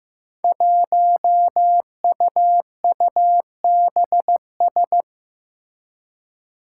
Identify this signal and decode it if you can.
Morse code, heard as a keyed tone, about 15 wpm, 700 Hz. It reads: 1UUBS
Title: Five Challenging Letter-Number Combinations 15wpm